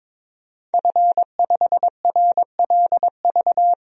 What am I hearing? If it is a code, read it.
Morse code, heard as a keyed tone, about 22 wpm, 700 Hz. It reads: F5RLV
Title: DXCC-100 Call Signs - 5 Characters - 22wpm